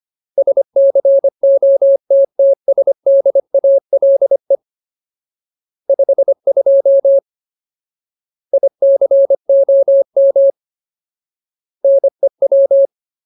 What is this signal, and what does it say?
Morse code, heard as a keyed tone, about 25 wpm, 550 Hz. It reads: SCOTTSDALE 52 ICOM NEW
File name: Ultimate Word List / 25WPM Ultimate